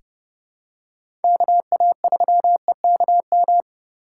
Morse code, heard as a keyed tone, about 31 wpm, 700 Hz. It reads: KA3EKM
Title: USA Call Signs - 6 Characters - 2x Character Spacing - 30wpm